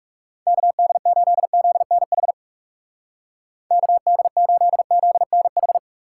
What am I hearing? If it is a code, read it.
Morse code, heard as a keyed tone, about 37 wpm, 700 Hz. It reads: KD8ZNH KD8ZNH
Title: Speed Racing - USA 2x3 Call Signs 30wpm